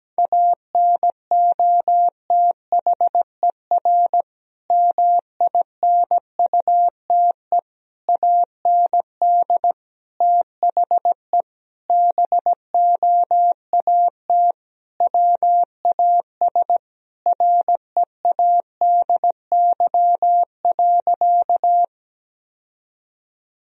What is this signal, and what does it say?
Morse code, heard as a keyed tone, about 17 wpm, 700 Hz. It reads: ANOTHER MINUTE AND THE BOAT WAS READY.
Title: Sentences from Top 1000 Words - Review Part 3 - 17wpm